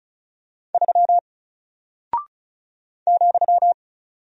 Morse code, heard as a keyed tone, about 36 wpm, 700 Hz. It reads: 3 ,
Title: Single Character (Letter:Number:Punctuation:Prosign) Alt 35wpm